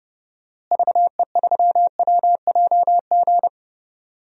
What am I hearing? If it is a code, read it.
Morse code, heard as a keyed tone, about 30 wpm, 700 Hz. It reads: VE3WJG
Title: DXCC-100 Call Signs - 6 Characters - Repeated 2x - 30wpm